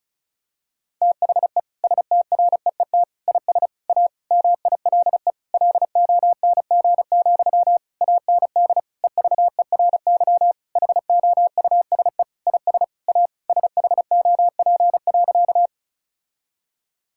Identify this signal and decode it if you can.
Morse code, heard as a keyed tone, about 34 wpm, 700 Hz. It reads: THE STREET IS A MILE LONG, AND EVERY HOUSE IS A SHOP.
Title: Sentences from Top 1600 Words 35wpm